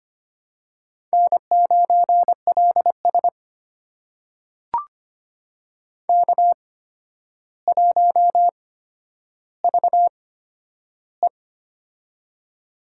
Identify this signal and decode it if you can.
Morse code, heard as a keyed tone, about 25 wpm, spaced as for 11 wpm, 700 Hz. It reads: N9LS K1VE
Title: USA Call Signs - 4 Characters - 8x Character Spacing - 25wpm